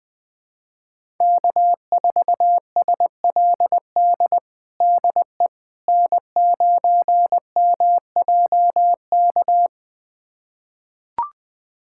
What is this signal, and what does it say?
Morse code, heard as a keyed tone, about 20 wpm, 700 Hz. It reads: K4SLD DE N9MJK
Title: USA Callsign DE Callsign - 5 Characters - 20wpm